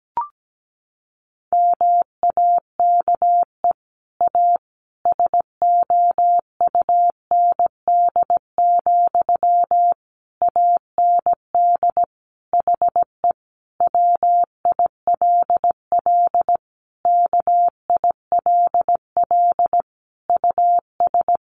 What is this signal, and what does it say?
Morse code, heard as a keyed tone, about 17 wpm, 700 Hz. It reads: MAKE A SOUND, AND HE WILL KILL US
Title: Sentences from Top 1400 Words 17wpm